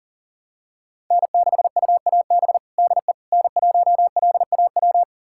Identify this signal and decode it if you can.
Morse code, heard as a keyed tone, about 40 wpm, 700 Hz. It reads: N6UAB DE N1LAW
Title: USA Callsign DE Callsign - 5 Characters - 40wpm